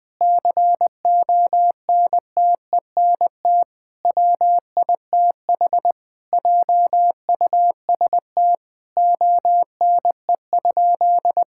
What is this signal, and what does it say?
Morse code, heard as a keyed tone, about 20 wpm, 700 Hz. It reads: CONTENT WITH JUST ONE?
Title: Sentences from Top 1000 Words - Review Part 6 - 20wpm